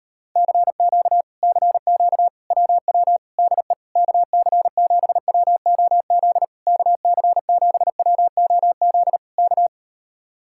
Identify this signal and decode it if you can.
Morse code, heard as a keyed tone, about 38 wpm, 700 Hz. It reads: CQ CQ WW DE KC7WOZ KC7WOZ K